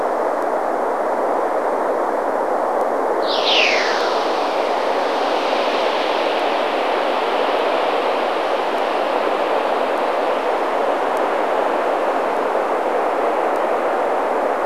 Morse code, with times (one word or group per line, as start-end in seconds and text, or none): none